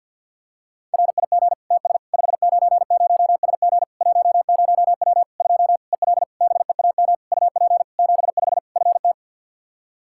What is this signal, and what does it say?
Morse code, heard as a keyed tone, about 50 wpm, 700 Hz. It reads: RIG TS 590SG 10W 2 EL BEAM UP 75 FT
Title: Practice QSO - Exchange 2 - Info - 50wpm